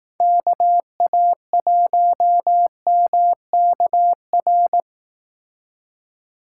Morse code, heard as a keyed tone, about 18 wpm, 700 Hz. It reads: KA1MKR
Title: Speed Racing - USA 2x3 Call Signs 15wpm